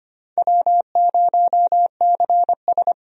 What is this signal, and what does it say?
Morse code, heard as a keyed tone, about 25 wpm, 700 Hz. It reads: W0CS
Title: USA Call Signs - 4 Characters - Repeated 2x - 25wpm